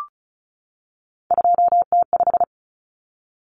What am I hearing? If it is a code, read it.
Morse code, heard as a keyed tone, about 34 wpm, 700 Hz. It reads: 2T5
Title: Common Cut Numbers - 3-Digit 35wpm